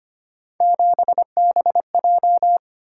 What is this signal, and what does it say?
Morse code, heard as a keyed tone, about 29 wpm, 700 Hz. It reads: 7BJ
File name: Speed-Racing - Three Letter-Number Combinations - 25wpm